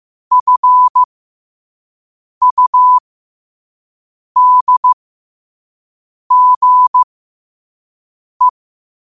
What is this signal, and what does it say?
Morse code, heard as a keyed tone, about 15 wpm, spaced as for 5 wpm, 1000 Hz. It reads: FUDGE
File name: qotdfiles / 5wpm